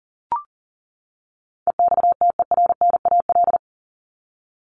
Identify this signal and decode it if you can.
Morse code, heard as a keyed tone, about 40 wpm, 700 Hz. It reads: EXTERNAL
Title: Top 1751-2000 Words 40wpm